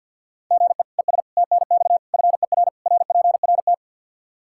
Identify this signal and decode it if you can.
Morse code, heard as a keyed tone, about 46 wpm, 700 Hz. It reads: GE ES TNX FER RPRT
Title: Practice QSO - Exchange 1 - Beginning - 50wpm